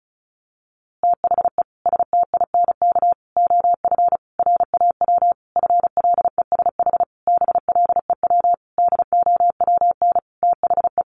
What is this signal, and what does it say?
Morse code, heard as a keyed tone, about 34 wpm, 700 Hz. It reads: THE STINK OF RAW FLESH BLEW DOWN THE
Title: A Sound Of Thunder 35wpm